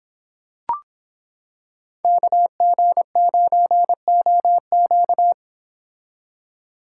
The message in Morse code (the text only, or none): KG9OQ